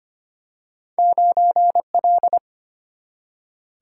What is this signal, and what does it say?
Morse code, heard as a keyed tone, about 25 wpm, 700 Hz. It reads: 9 L
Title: Single-Letter Single-Number 25wpm